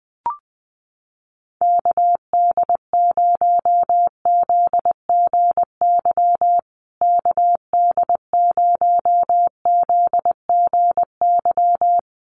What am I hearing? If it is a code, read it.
Morse code, heard as a keyed tone, about 20 wpm, 700 Hz. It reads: KD0ZGY KD0ZGY
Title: USA Call Signs Repeated 3x 20wpm